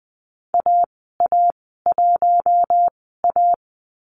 Morse code, heard as a keyed tone, about 20 wpm, spaced as for 14 wpm, 700 Hz. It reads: AA1A
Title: 055 Lesson 10g - Letter r - Call Signs 20wpm